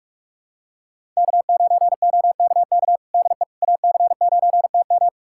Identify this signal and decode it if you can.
Morse code, heard as a keyed tone, about 45 wpm, 700 Hz. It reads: K9OKK DE AC9TM
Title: USA Callsign DE Callsign - 5 Characters - 45wpm